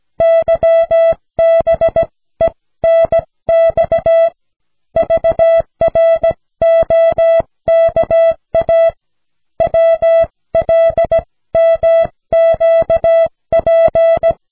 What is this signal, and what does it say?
Morse code, heard as a keyed tone, about 17 wpm, spaced as for 15 wpm, 660 Hz. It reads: YBENX VROKA WLMQP
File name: cw / snelheid 4